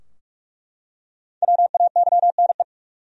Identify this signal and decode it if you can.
Morse code, heard as a keyed tone, about 42 wpm, 700 Hz. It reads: WAYNE
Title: USA Common Names - 45wpm